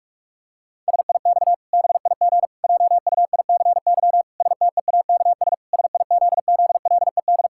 Morse code, heard as a keyed tone, about 46 wpm, 700 Hz. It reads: SIX BIG JUICY STEAKS SIZZLED
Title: Pangram Challenge v2 - 45wpm